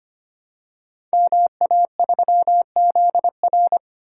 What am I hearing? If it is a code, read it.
Morse code, heard as a keyed tone, about 25 wpm, 700 Hz. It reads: MA3ZR
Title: Code Groups - 5 Letter:Numbers - 25wpm